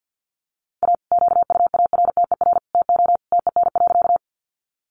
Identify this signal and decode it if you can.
Morse code, heard as a keyed tone, about 50 wpm, 700 Hz. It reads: A QUARTER TO TEN.
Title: Sentences from Top 1300 Words 50wpm